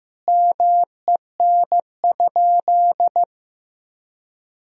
Morse code, heard as a keyed tone, about 16 wpm, 700 Hz. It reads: MEN?
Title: Sentences from Top 200 Words and Abbreviations 15wpm